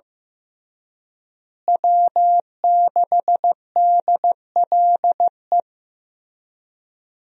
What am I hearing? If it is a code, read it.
Morse code, heard as a keyed tone, about 15 wpm, 700 Hz. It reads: W6DLE
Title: USA Call Signs - 5 Characters - Repeated 3x - 15wpm